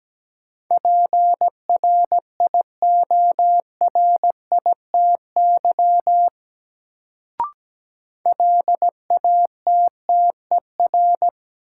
Morse code, heard as a keyed tone, about 17 wpm, 700 Hz. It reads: PRIORITY LATTER
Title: Sentences from Top 1600 Words - New Word Drill - 17wpm